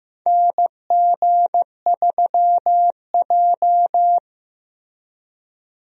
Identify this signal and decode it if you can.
Morse code, heard as a keyed tone, about 15 wpm, 700 Hz. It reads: NG3J
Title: USA Call Signs - 4 Characters - Repeated 2x - 15wpm